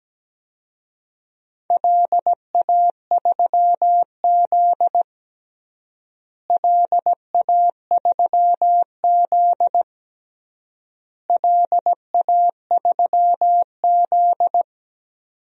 Morse code, heard as a keyed tone, about 17 wpm, 700 Hz. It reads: LA3Z LA3Z LA3Z
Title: DXCC-100 Call Signs - 4 Characters - Repeated 3x - 17wpm